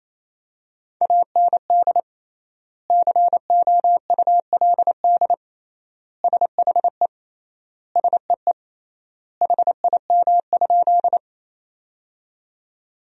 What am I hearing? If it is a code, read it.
Morse code, heard as a keyed tone, about 28 wpm, 700 Hz. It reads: AND COULD SHE SEE HIM?
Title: 3x Word-Spacing - Sentences from Top 100 Words - 28wpm